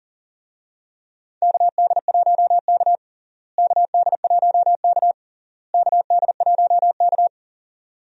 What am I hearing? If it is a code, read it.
Morse code, heard as a keyed tone, about 40 wpm, 700 Hz. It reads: KD1K KD1K KD1K
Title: USA Call Signs - 4 Characters - Repeated 3x - 40wpm